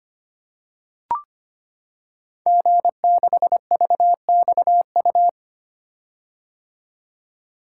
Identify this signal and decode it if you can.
Morse code, heard as a keyed tone, about 25 wpm, 700 Hz. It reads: G6VXU
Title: Code Groups - 5 Letter:Numbers - 25wpm